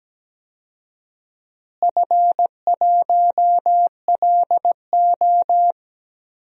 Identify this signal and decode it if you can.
Morse code, heard as a keyed tone, about 17 wpm, 700 Hz. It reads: F1LO
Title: DXCC-100 Call Signs - 4 Characters - Repeated 2x - 17wpm